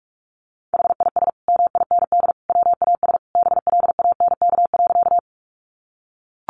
Morse code, heard as a keyed tone, about 44 wpm, 700 Hz. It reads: HIS MIND WAS BLANK.
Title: Sentences from Top 1500 Words 45wpm